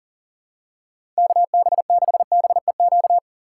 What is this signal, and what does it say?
Morse code, heard as a keyed tone, about 40 wpm, 700 Hz. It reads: KB6BEQ
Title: USA Call Signs - 6 Characters - 3x Character Spacing - 40wpm